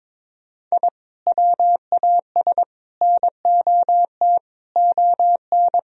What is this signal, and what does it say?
Morse code, heard as a keyed tone, about 22 wpm, 700 Hz. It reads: I WAS NOT ON
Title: Sentences from Top 1000 Words - Review Part 4 - 22wpm